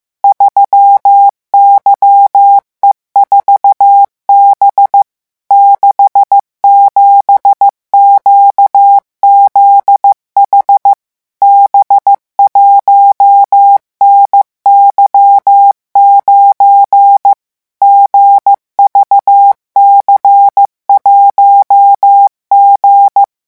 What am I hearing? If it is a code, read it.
Morse code, heard as a keyed tone, about 15 wpm, 800 Hz. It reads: 3YE4B 67QZH B1NY9 GVC1G
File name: morse mp3 / revision12-vitesse 14 mots